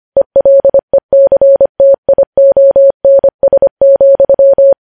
Morse code, heard as a keyed tone, about 25 wpm, 550 Hz. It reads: ELECTIONS,
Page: news headlines in morse code at 25 wpm